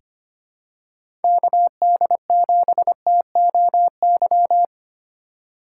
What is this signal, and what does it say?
Morse code, heard as a keyed tone, about 25 wpm, 700 Hz. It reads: KD7TOY
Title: USA Call Signs - 4x Character Spacing - 25wpm